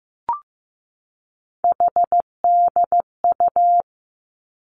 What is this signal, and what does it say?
Morse code, heard as a keyed tone, about 15 wpm, 700 Hz. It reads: HDU
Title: Three Challenging Letter-Number Combinations 15wpm